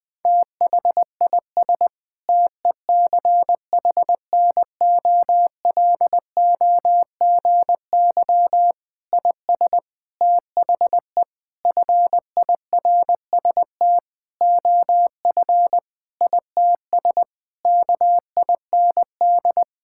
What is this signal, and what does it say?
Morse code, heard as a keyed tone, about 21 wpm, 700 Hz. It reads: THIS TECHNOLOGY IS THE FIRST OF ITS KIND
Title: Sentences with 8-Word Phrases - 20wpm